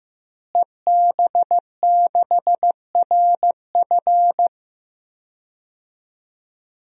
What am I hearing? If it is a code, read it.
Morse code, heard as a keyed tone, about 15 wpm, 700 Hz. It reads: EB6RF
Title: DXCC-100 Call Signs - 5 Characters - Repeated 3x - 15wpm